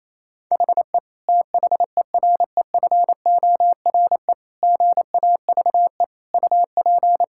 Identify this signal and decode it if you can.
Morse code, heard as a keyed tone, about 28 wpm, 700 Hz. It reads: HE THEREFORE GAVE UP
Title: Sentences from Top 1000 Words - Review Part 5 - 28wpm